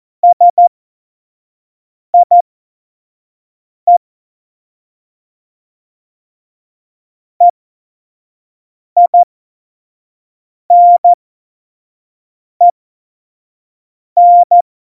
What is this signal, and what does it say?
Morse code, heard as a keyed tone, about 14 wpm, spaced as for 5 wpm, 700 Hz. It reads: SIE EINEN
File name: Morsekurs / Witze / Witz 15. Oktober 2019, 25 F70